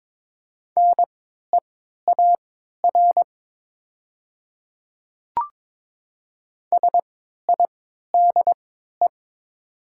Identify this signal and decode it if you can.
Morse code, heard as a keyed tone, about 22 wpm, spaced as for 12 wpm, 700 Hz. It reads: NEAR SIDE
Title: 3x Character Spacing - Top 200 Words - 22wpm